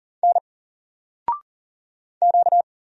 Morse code, heard as a keyed tone, about 38 wpm, 700 Hz. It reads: N Q
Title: Single Character (Letter:Number:Punctuation:Prosign) 40wpm